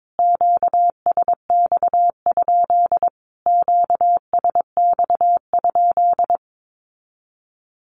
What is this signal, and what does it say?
Morse code, heard as a keyed tone, about 22 wpm, 700 Hz. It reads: QSX? QSX?
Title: Q-Codes Repeated 3x 22wpm